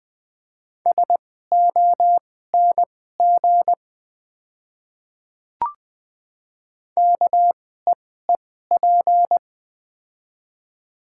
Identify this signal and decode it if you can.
Morse code, heard as a keyed tone, about 20 wpm, spaced as for 14 wpm, 690 Hz. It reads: SONG KEEP
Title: 2x Character Spacing - Top 500 Words - 20wpm